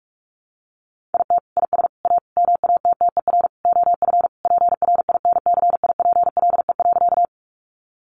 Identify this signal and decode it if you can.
Morse code, heard as a keyed tone, about 44 wpm, 700 Hz. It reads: IT IS A MATTER OF PRINCIPLE.
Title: Sentences from Top 1400 Words 45wpm